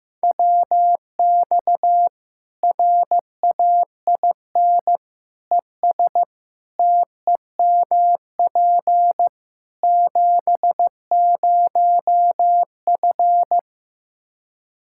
Morse code, heard as a keyed tone, about 16 wpm, 700 Hz. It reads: WX RAIN ES TEMP 70F
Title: Practice QSO - Exchange 2 - Info - 15wpm